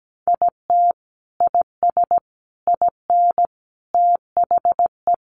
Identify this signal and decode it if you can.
Morse code, heard as a keyed tone, about 17 wpm, 700 Hz. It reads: IT IS IN THE
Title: Sentences from Top 1500 Words 17wpm